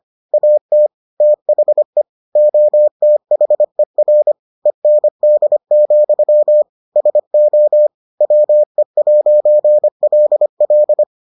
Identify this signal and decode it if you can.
Morse code, heard as a keyed tone, about 25 wpm, 600 Hz. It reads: AT THE OTHER END, SO WE'LL